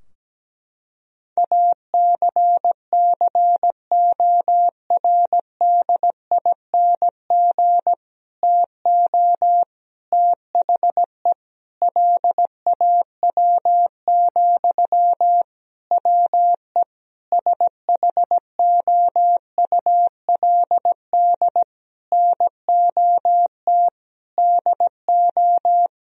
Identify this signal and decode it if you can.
Morse code, heard as a keyed tone, about 17 wpm, 700 Hz. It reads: ACCORDING TO THE LAW, WE SHOULD NOT DO
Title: Sentences from Top 1000 Words - Review Part 4 - 17wpm